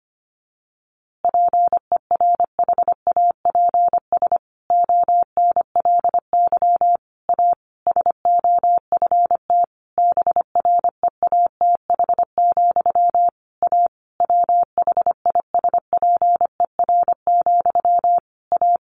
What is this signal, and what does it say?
Morse code, heard as a keyed tone, about 25 wpm, 700 Hz. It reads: PERHAPS ONLY A SOFT BREATH, A WHISPER, A
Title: A Sound Of Thunder 25wpm